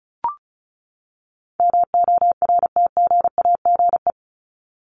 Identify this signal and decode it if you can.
Morse code, heard as a keyed tone, about 34 wpm, 700 Hz. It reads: MORTGAGE
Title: Top 1751-2000 Words 35wpm